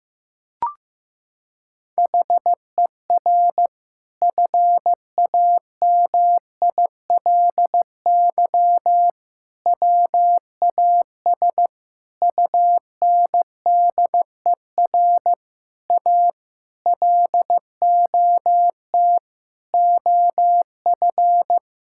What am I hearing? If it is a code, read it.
Morse code, heard as a keyed tone, about 16 wpm, 700 Hz. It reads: HER FAMILY WAS UNDER A LOT OF
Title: Sentences from Top 1000 Words - Review Part 6 - 15wpm